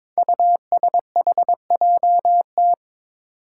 Morse code, heard as a keyed tone, about 21 wpm, 700 Hz. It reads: USHJT
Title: Five Challenging Letter-Number Combinations 22wpm